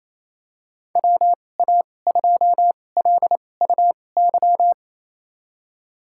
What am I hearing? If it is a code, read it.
Morse code, heard as a keyed tone, about 28 wpm, spaced as for 20 wpm, 700 Hz. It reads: WA2LUY
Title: USA Call Signs - 6 Characters - 2x Character Spacing - 28wpm